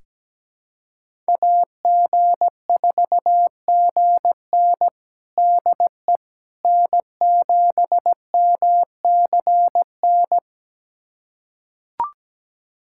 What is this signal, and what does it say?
Morse code, heard as a keyed tone, about 17 wpm, 700 Hz. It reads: AG4GN DE N7MCN
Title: USA Callsign DE Callsign - 5 Characters - 17wpm